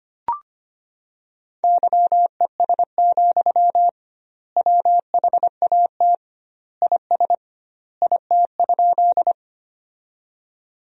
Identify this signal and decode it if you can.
Morse code, heard as a keyed tone, about 25 wpm, 700 Hz. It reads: YES, WHAT IS IT?
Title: 2x Word-Spacing - Sentences from Top 500 Words - 25wpm